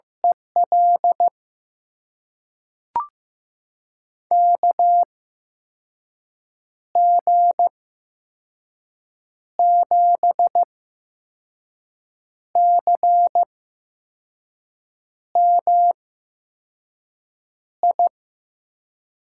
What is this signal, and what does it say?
Morse code, heard as a keyed tone, about 16 wpm, spaced as for 5 wpm, 700 Hz. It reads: EL KG7CMI
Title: USA Call Signs - 6 Characters - 8x Character Spacing - 15wpm